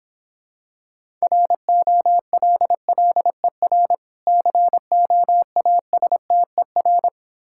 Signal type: Morse code

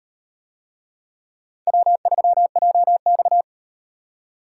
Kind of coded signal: Morse code

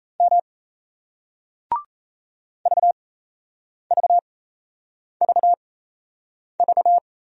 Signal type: Morse code

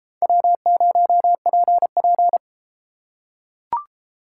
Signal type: Morse code

